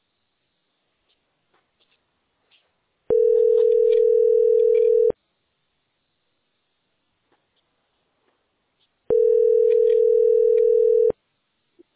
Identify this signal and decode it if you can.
Ringback tone (North America), 440 + 480 Hz.
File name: 7d43bec8-a07e-40f7-9eea